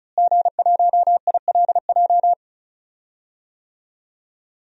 Morse code, heard as a keyed tone, about 36 wpm, 700 Hz. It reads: G1ILJ
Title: DXCC-100 Call Signs - 5 Characters - Repeated 2x - 35wpm